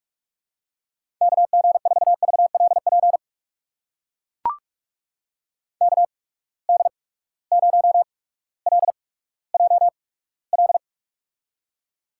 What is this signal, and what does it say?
Morse code, heard as a keyed tone, about 44 wpm, spaced as for 19 wpm, 700 Hz. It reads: KG4VLP KD0LJL
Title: USA Call Signs - 6 Characters - 8x Character Spacing - 45wpm